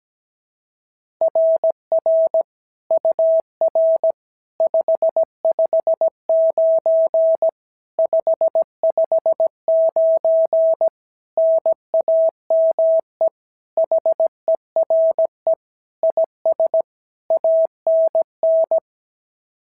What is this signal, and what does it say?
Morse code, heard as a keyed tone, about 16 wpm, 650 Hz. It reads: RR UR 559 559 NAME HERE IS ANN
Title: SOTA QSO Practice 17wpm